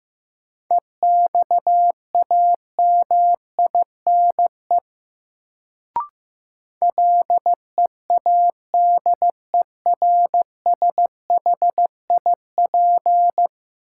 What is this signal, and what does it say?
Morse code, heard as a keyed tone, about 15 wpm, 700 Hz. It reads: EXAMINE LEADERSHIP
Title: Sentences from Top 1600 Words - New Word Drill - 15wpm